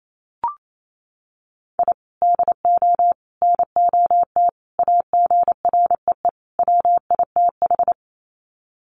Morse code, heard as a keyed tone, about 28 wpm, 700 Hz. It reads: I DO NOT AGREE WITH
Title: Sets of 5 Words Encore 28wpm